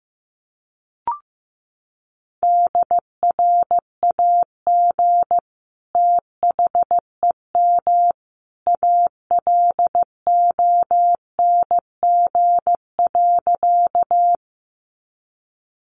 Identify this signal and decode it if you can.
Morse code, heard as a keyed tone, about 15 wpm, 700 Hz. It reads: DRAG THEM ALONG.
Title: Sentences from Top 1500 Words 15wpm